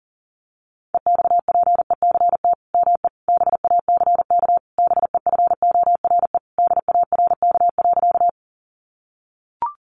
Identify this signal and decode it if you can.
Morse code, heard as a keyed tone, about 37 wpm, 700 Hz. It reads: EXPECT ME BACK BEFORE DARK.
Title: Sentences from Top 1100 Words 40wpm